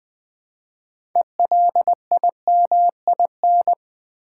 Morse code, heard as a keyed tone, about 20 wpm, 700 Hz. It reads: ELIMIN
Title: Build A Word - 3 Letters At A Time - 9-Letter Words - 20wpm